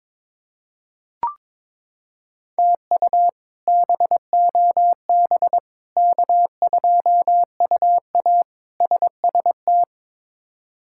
Morse code, heard as a keyed tone, about 22 wpm, 700 Hz. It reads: TU BOB K2UA SST
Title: K1USN SST QSO Practice - 22wpm